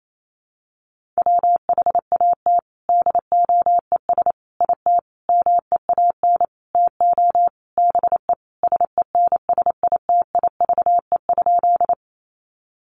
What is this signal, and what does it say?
Morse code, heard as a keyed tone, about 28 wpm, 700 Hz. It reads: WHAT DOES IT MEAN TO BE SENSITIVE?
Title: Sentences from Top 1400 Words 28wpm